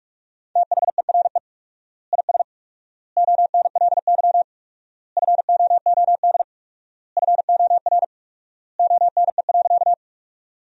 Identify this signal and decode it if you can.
Morse code, heard as a keyed tone, about 46 wpm, 700 Hz. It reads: THERE IS ONLY FOOD FOR ONE.
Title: 4x Word-Spacing - Sentences from Top 300 Words - 45wpm